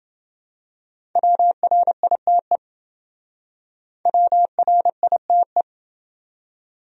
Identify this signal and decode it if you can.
Morse code, heard as a keyed tone, about 29 wpm, 700 Hz. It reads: WRITE WRITE